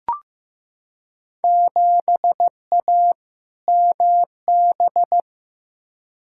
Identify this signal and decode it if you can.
Morse code, heard as a keyed tone, about 15 wpm, 700 Hz. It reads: 7A MB
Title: Field Day Exchanges Encore - 15wpm